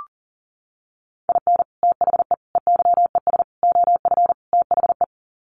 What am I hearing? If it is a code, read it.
Morse code, heard as a keyed tone, about 40 wpm, 700 Hz. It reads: IN THE EYES OF THE
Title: Sets of 5 Words 40wpm